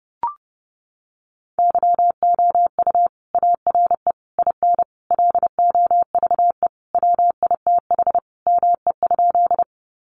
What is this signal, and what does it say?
Morse code, heard as a keyed tone, about 30 wpm, 700 Hz. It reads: YOU ARE IN LOVE WITH ME?
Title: Sentences from Top 400 Words 30wpm